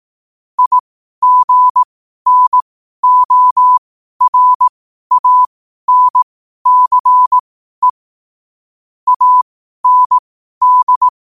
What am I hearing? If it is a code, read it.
Morse code, heard as a keyed tone, about 18 wpm, spaced as for 12 wpm, 1000 Hz. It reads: IGNORANCE AND